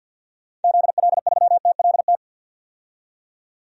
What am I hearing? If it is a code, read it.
Morse code, heard as a keyed tone, about 45 wpm, 700 Hz. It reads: ZL3TLT
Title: DXCC-100 Call Signs - 6 Characters - 50wpm